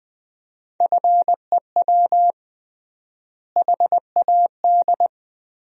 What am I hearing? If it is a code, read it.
Morse code, heard as a keyed tone, about 20 wpm, 700 Hz. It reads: FEW HAD
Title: 3x Word-Spacing - Sentences from Top 300 Words - 20wpm